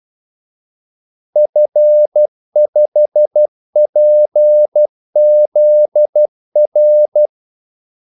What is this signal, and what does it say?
Morse code, heard as a keyed tone, about 12 wpm, 600 Hz. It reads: F5PZR
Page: Kaldesignaler 12 wpm